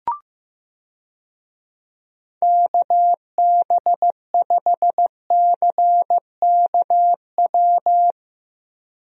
Morse code, heard as a keyed tone, about 15 wpm, 700 Hz. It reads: KB5CKW
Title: USA Call Signs - 6 Characters - Repeated 2x - 15wpm